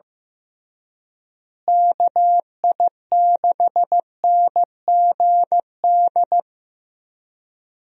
Morse code, heard as a keyed tone, about 15 wpm, 700 Hz. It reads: KI6NGD
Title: USA Call Signs - 6 Characters - Repeated 3x - 15wpm